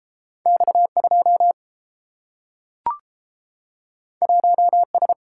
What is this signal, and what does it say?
Morse code, heard as a keyed tone, about 30 wpm, 700 Hz. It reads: X2 1S